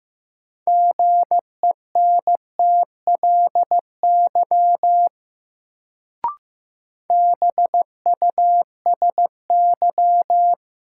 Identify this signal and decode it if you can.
Morse code, heard as a keyed tone, about 15 wpm, 700 Hz. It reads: GENTLY BUSY
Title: Sentences from Top 1500 Words - New Word Drill Alt - 15wpm